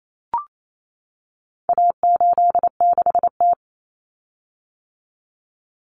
Morse code, heard as a keyed tone, about 28 wpm, 700 Hz. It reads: A86T
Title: Common Cut Numbers - 4-Digit 28wpm